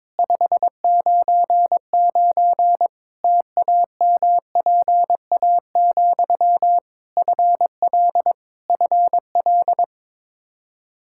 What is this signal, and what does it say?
Morse code, heard as a keyed tone, about 22 wpm, 700 Hz. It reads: 599 TAMPA, FL FL
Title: K1USN SST QSO Practice 22wpm